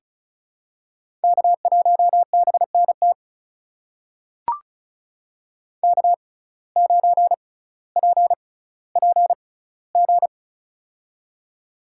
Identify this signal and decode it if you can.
Morse code, heard as a keyed tone, about 36 wpm, spaced as for 18 wpm, 700 Hz. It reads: K1BNT K9PPG
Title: USA Call Signs - 5 Characters - 6x Character Spacing - 35wpm